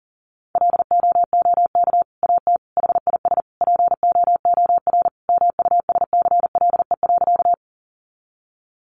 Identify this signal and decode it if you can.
Morse code, heard as a keyed tone, about 40 wpm, 700 Hz. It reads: LOOK AT HIS POOR MUSCLE.
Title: Sentences from Top 1300 Words 40wpm